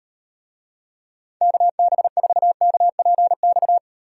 Morse code, heard as a keyed tone, about 38 wpm, 700 Hz. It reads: KB4KPX